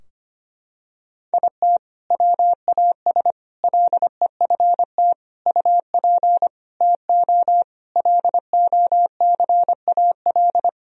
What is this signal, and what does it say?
Morse code, heard as a keyed tone, about 25 wpm, 700 Hz. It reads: IT WAS LEFT UP TO LOCAL
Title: Sentences from Top 1000 Words - Review Part 4 - 25wpm